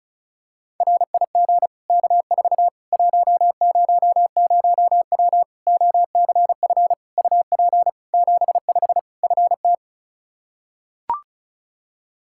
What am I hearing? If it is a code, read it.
Morse code, heard as a keyed tone, about 34 wpm, 700 Hz. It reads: RIG K4 100W OCF UP 75 FT
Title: Practice QSO - Exchange 2 - Info - 35wpm